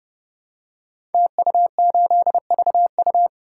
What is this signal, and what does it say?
Morse code, heard as a keyed tone, about 30 wpm, 700 Hz. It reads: TU8VU
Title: Code Groups - 5 Letter:Numbers - 30wpm